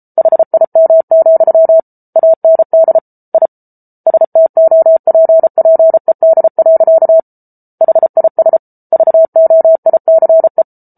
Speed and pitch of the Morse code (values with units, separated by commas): 34 wpm, 650 Hz